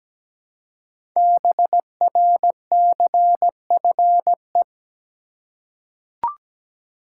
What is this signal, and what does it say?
Morse code, heard as a keyed tone, about 17 wpm, 700 Hz. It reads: BRCFE
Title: Code Groups - 5 Letters - 17wpm